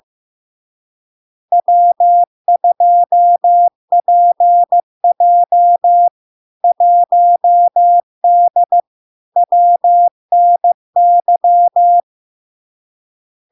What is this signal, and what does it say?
Morse code, heard as a keyed tone, about 15 wpm, 700 Hz. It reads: W2PJ 1D WNY
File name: LARK-FD-2022-Log-in-CW-15wpm-700-Hz